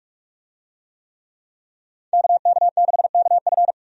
Morse code, heard as a keyed tone, about 44 wpm, 700 Hz. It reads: KK6KF
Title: USA Call Signs - 5 Characters - 45wpm